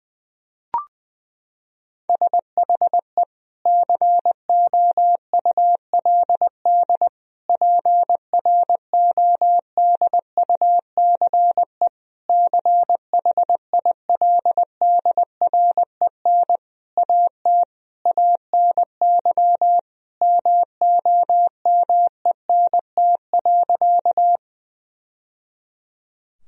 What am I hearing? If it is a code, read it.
Morse code, heard as a keyed tone, about 20 wpm, 700 Hz. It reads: SHE COULD PRODUCE CHILDREN AT ANY MOMENT.
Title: Sentences from Top 1000 Words - Review Part 3 - 20wpm